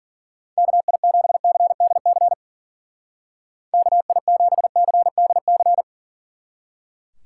Morse code, heard as a keyed tone, about 48 wpm, 700 Hz. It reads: KI7CDC KI7CDC
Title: Speed Racing - USA 2x3 Call Signs 40wpm